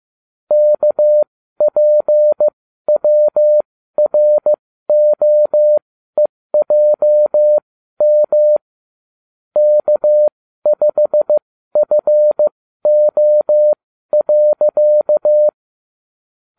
Morse code, heard as a keyed tone, about 15 wpm, spaced as for 12 wpm, 600 Hz. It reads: KPWROEJM K5FO.